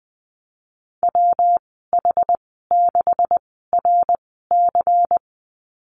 Morse code, heard as a keyed tone, about 20 wpm, spaced as for 15 wpm, 700 Hz. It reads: WH6RC
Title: 167 Lesson 27g - Number 7 - Call Signs 20wpm